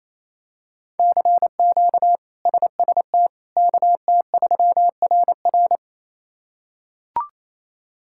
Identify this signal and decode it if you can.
Morse code, heard as a keyed tone, about 28 wpm, 700 Hz. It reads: CQ SST KT3RR
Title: K1USN SST QSO Practice - 28wpm